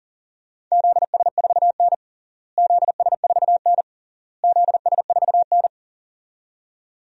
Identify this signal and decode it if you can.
Morse code, heard as a keyed tone, about 40 wpm, 700 Hz. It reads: ZS4N ZS4N ZS4N
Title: DXCC-100 Call Signs - 4 Characters - Repeated 3x - 40wpm